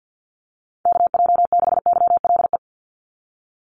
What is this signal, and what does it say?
Morse code, heard as a keyed tone, about 50 wpm, 700 Hz. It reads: KJ6YLE
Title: USA Call Signs 50wpm